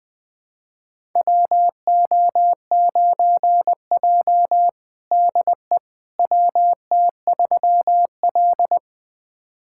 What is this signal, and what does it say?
Morse code, heard as a keyed tone, about 20 wpm, 700 Hz. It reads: WO9J DE WT3L
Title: USA Callsign DE Callsign - 4 Characters - 20wpm